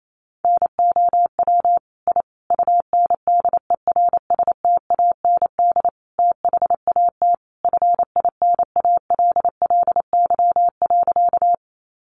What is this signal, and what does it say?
Morse code, heard as a keyed tone, about 28 wpm, 700 Hz. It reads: NOW I UNDERSTAND THAT FINALLY.
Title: Sentences from Top 1000 Words 28wpm